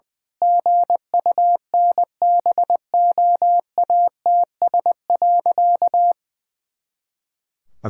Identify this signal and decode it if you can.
Morse code, heard as a keyed tone, about 20 wpm, 700 Hz. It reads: GUNBOATS.
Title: Pangram Challenge v2 - 20wpm